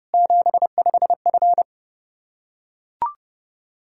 Morse code, heard as a keyed tone, about 31 wpm, 700 Hz. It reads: 75F
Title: Practice QSO - Exchange 2 - Info - 30wpm